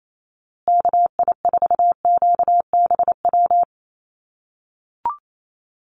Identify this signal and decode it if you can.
Morse code, heard as a keyed tone, about 28 wpm, 700 Hz. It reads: KI4QBW
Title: USA Call Signs Repeated 2x 28wpm